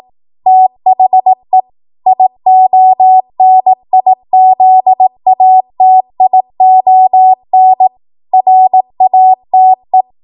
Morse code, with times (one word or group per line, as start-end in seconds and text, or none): 0.5-1.6 THE
2.1-7.9 IONIZATION
8.3-10.0 RATE